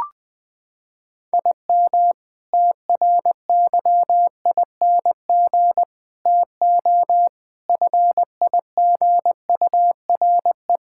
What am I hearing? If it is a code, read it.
Morse code, heard as a keyed tone, about 20 wpm, 700 Hz. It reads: IM TRYING TO FIGURE
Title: Sentences with 6-Word Phrases - 20wpm